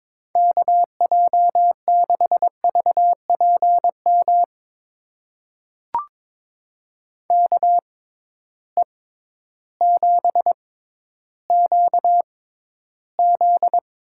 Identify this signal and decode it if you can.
Morse code, heard as a keyed tone, about 22 wpm, spaced as for 12 wpm, 700 Hz. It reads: KJ6VPM KE7QZ
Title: USA Call Signs - 6 Characters - 6x Character Spacing - 22wpm